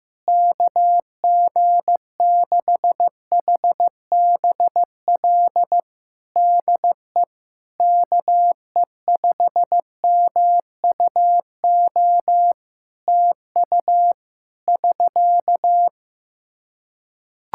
Morse code, heard as a keyed tone, about 15 wpm, 700 Hz. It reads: KG6HBL DE KE5MUO TU <SK>
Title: Practice QSO - Exchange 4 - Ending - 15wpm